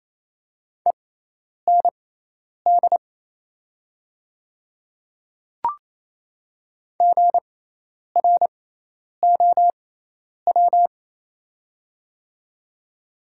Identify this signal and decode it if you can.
Morse code, heard as a keyed tone, about 28 wpm, spaced as for 10 wpm, 700 Hz. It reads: END GROW
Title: 6x Character Spacing - Top 300 Words - 28wpm